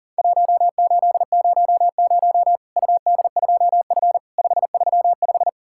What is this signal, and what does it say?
Morse code, heard as a keyed tone, about 43 wpm, 700 Hz. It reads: 1800 UD2F 535
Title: Worked All Europe DX Contest - QTC Exchange - 40wpm